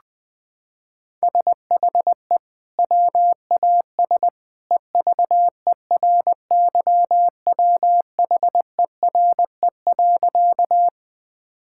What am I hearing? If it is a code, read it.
Morse code, heard as a keyed tone, about 20 wpm, 700 Hz. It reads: SHE WAS EVERYWHERE.
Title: Sentences from Top 1600 Words 20wpm